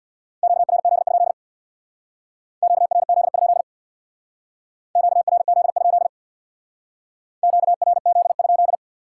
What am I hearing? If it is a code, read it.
Morse code, heard as a keyed tone, about 63 wpm, 700 Hz. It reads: QRZ? QRZ? QRZ? QRZ?